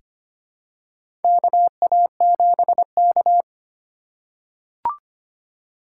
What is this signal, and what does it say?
Morse code, heard as a keyed tone, about 25 wpm, 700 Hz. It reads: KA7K
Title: USA Call Signs - 3x Character Spacing - 25wpm